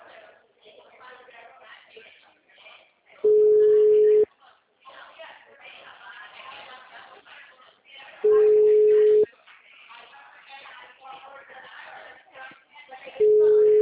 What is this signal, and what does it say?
Ringback tone (Europe, 425 Hz), with speech.